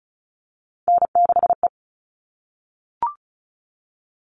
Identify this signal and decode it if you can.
Morse code, heard as a keyed tone, about 34 wpm, 700 Hz. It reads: N6E
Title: All Cut Numbers - 3-Digit 35wpm